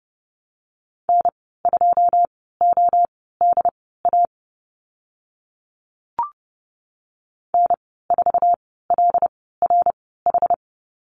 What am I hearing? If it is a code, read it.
Morse code, heard as a keyed tone, about 30 wpm, spaced as for 17 wpm, 700 Hz. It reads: N2ODA N4LRH
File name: Lesson 15g - Number 5 - Call Signs 30wpm